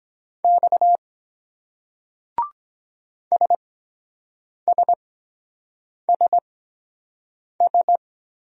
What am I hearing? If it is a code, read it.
Morse code, heard as a keyed tone, about 23 wpm, spaced as for 7 wpm, 700 Hz. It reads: X SSSS